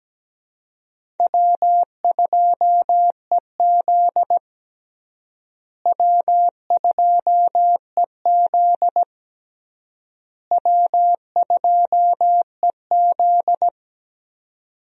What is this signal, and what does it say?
Morse code, heard as a keyed tone, about 17 wpm, 700 Hz. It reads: W2EZ W2EZ W2EZ
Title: USA Call Signs - 4 Characters - Repeated 3x - 17wpm